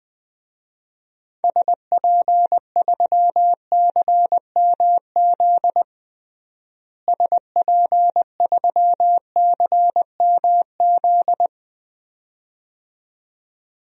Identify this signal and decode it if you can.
Morse code, heard as a keyed tone, about 20 wpm, 700 Hz. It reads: SP3CMZ SP3CMZ
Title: DXCC-100 Call Signs - 6 Characters - Repeated 2x - 20wpm